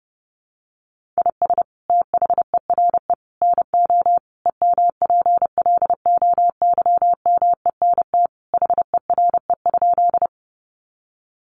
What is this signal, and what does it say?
Morse code, heard as a keyed tone, about 30 wpm, 700 Hz. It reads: IS THERE NO EMPLOYMENT HERE?
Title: Sentences from Top 1500 Words 30wpm